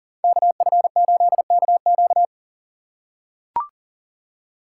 Morse code, heard as a keyed tone, about 38 wpm, 700 Hz. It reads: KF8KQ
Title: USA Call Signs - 5 Characters - 2x Character Spacing - 40wpm